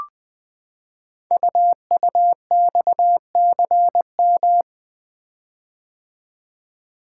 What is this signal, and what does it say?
Morse code, heard as a keyed tone, about 20 wpm, 700 Hz. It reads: UUXCM
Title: Code Groups - 5 Letters - 20wpm